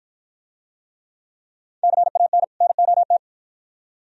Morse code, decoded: CAN NOT

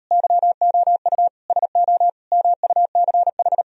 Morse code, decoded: YOU SO MUCH

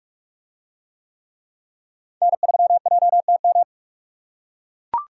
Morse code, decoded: N3JTM

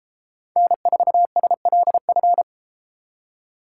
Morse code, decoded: N4SLF

